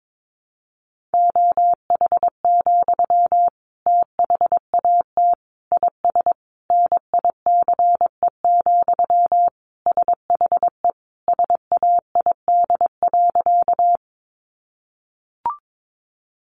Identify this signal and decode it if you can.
Morse code, heard as a keyed tone, about 22 wpm, 700 Hz. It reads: OH, THAT IS NICE, SHE SAID.
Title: Sentences from Top 1100 Words 22wpm